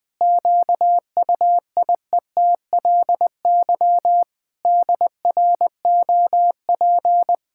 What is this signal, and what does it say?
Morse code, heard as a keyed tone, about 20 wpm, 700 Hz. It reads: QUIETLY DROP
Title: Pangram Challenge v2 - 20wpm